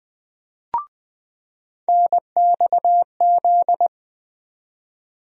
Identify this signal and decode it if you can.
Morse code, heard as a keyed tone, about 20 wpm, 700 Hz. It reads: NXZ
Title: USA Call Sign Postfixes - 3 Letters 20wpm